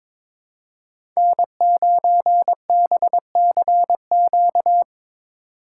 Morse code, heard as a keyed tone, about 21 wpm, 700 Hz. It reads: N9BCQ
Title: USA Call Signs - 5 Characters - 2x Character Spacing - 22wpm